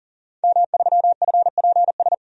Morse code, heard as a keyed tone, about 39 wpm, 700 Hz. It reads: M3FPS
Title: DXCC-100 Call Signs - 5 Characters - 40wpm